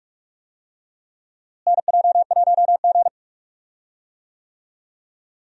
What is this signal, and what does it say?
Morse code, heard as a keyed tone, about 42 wpm, 700 Hz. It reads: NJ1G
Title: USA Call Signs - 4 Characters - Repeated 2x - 45wpm